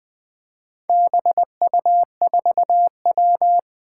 Morse code, heard as a keyed tone, about 20 wpm, 700 Hz. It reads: BU4W
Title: Four Challenging Letter-Number Combinations 20wpm